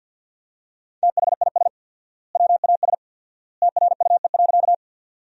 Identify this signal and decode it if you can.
Morse code, heard as a keyed tone, about 50 wpm, 700 Hz. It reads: THIS WAS TRUE.
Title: 4x Word-Spacing - Sentences from Top 400 Words - 50wpm